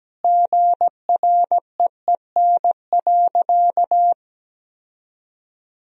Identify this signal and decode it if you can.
Morse code, heard as a keyed tone, about 17 wpm, 700 Hz. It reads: GREEN.
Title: Sentences from Top 1000 Words - Review Part 2 - 17wpm